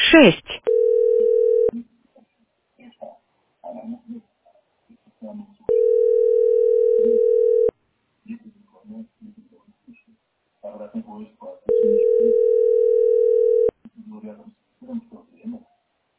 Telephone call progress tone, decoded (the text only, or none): ringback tone